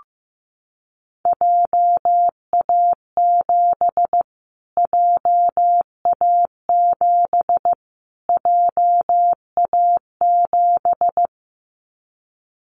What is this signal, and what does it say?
Morse code, heard as a keyed tone, about 15 wpm, 700 Hz. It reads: JA7 JA7 JA7
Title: Callsign Prefixes for Top 25 DXCC - Repeated 3x - 15wpm